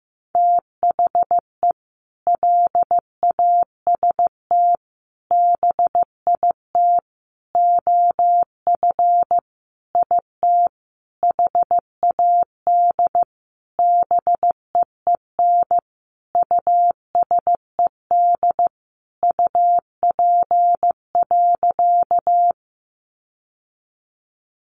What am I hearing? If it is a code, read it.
Morse code, heard as a keyed tone, about 15 wpm, 700 Hz. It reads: THE LAST BIT OF IT HAD BEEN USED UP.
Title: Sentences from Top 900 Words 15wpm